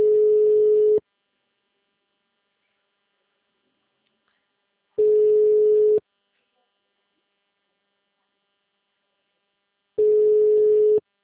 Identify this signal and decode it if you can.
Ringback tone (Europe, 425 Hz).
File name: out- -2020- - - .58